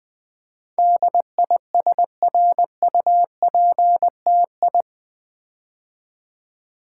Morse code, heard as a keyed tone, about 20 wpm, 700 Hz. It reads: DISRUPTI
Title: Build A Word - 4 Letters At A Time - 10-Letter Words - 20wpm